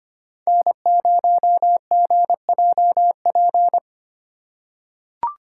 Morse code, heard as a keyed tone, about 25 wpm, 700 Hz. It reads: N0GJP
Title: USA Call Signs - 5 Characters - 2x Character Spacing - 25wpm